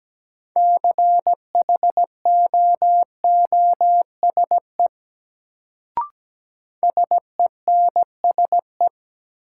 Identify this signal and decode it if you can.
Morse code, heard as a keyed tone, about 17 wpm, 700 Hz. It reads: CHOOSE SENSE
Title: Sentences from Top 700 Words - New Word Drill - 17wpm